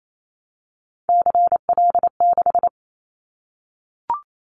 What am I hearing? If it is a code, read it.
Morse code, heard as a keyed tone, about 28 wpm, 700 Hz. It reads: CL6
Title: Callsign Prefixes for Top 100 DXCC - 28wpm